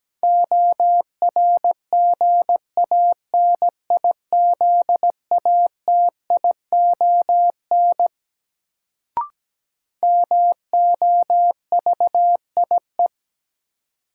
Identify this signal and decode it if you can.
Morse code, heard as a keyed tone, about 17 wpm, 700 Hz. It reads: ORGANIZATION MOVIE
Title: Sentences from Top 1000 Words - Review Part 5 - New Word Drill - 17wpm